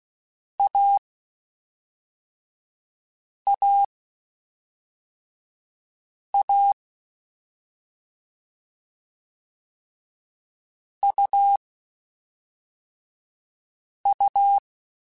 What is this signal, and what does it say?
Morse code, heard as a keyed tone, about 16 wpm, 800 Hz. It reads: A A A U U
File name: Short Runs / Short Runs-mp3s / AUV AUV4 @5